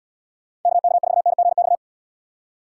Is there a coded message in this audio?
Yes: Morse code from 0.7 to 1.8 s, heard as a keyed tone, about 77 wpm, 700 Hz. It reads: KK4NGX